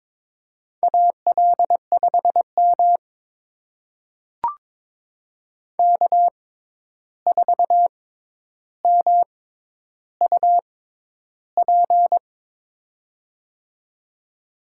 Morse code, heard as a keyed tone, about 21 wpm, spaced as for 10 wpm, 700 Hz. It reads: AL5M K4MUP
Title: USA Call Signs - 6x Character Spacing - 22wpm